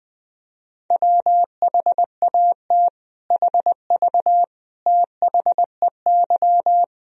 Morse code, heard as a keyed tone, about 20 wpm, 700 Hz. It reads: WHAT HV THEY
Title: Sentences from Top 500 Words and Abbreviations 20wpm